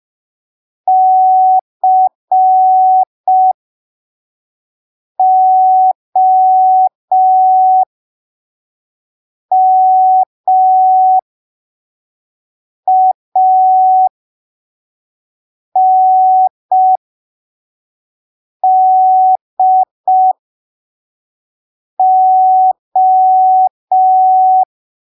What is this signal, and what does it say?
Morse code, heard as a keyed tone, about 5 wpm, spaced as for 3 wpm, 750 Hz. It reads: COMANDO
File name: arquivos / texto / texto 13 05ppm